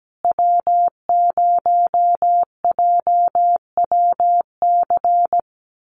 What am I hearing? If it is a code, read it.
Morse code, heard as a keyed tone, about 17 wpm, 700 Hz. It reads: W0JWC
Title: USA Call Signs Repeated 2x 17wpm